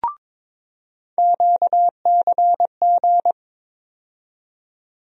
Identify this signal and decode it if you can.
Morse code, heard as a keyed tone, about 22 wpm, 700 Hz. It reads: QCG
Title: USA Call Sign Postfixes - 3 Letters 22wpm